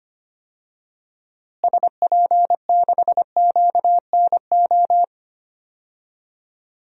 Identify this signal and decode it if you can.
Morse code, heard as a keyed tone, about 25 wpm, 700 Hz. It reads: SP6QNO
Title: DXCC-100 Call Signs - 6 Characters - Repeated 3x - 25wpm